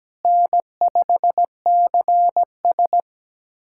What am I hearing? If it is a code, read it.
Morse code, heard as a keyed tone, about 17 wpm, 700 Hz. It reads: N5CS
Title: USA Callsign DE Callsign - 4 Characters - 17wpm